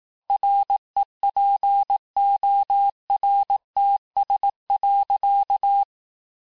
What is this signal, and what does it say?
Morse code, heard as a keyed tone, about 18 wpm, 800 Hz. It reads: REPORTS.
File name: cours-du-lundi files / cours-1594--FIN-de-QSO-vitesse-17mpm